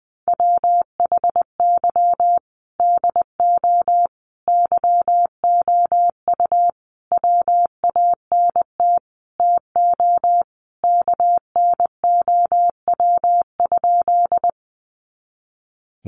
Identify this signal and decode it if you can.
Morse code, heard as a keyed tone, about 20 wpm, 700 Hz. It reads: WHY DO YOU WANT TO KNOW?
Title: Sentences from Top 200 Words 20wpm